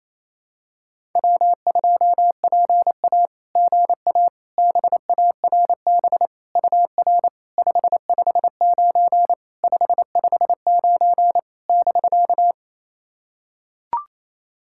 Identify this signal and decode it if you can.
Morse code, heard as a keyed tone, about 28 wpm, 690 Hz. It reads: W2PA GA BARB UR 559 559 <BK>
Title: SOTA QSO Practice 28wpm